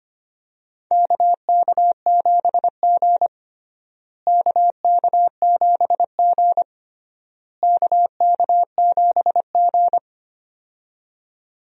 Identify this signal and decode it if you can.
Morse code, heard as a keyed tone, about 25 wpm, 700 Hz. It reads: KK7G KK7G KK7G
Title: USA Call Signs - 4 Characters - Repeated 3x - 25wpm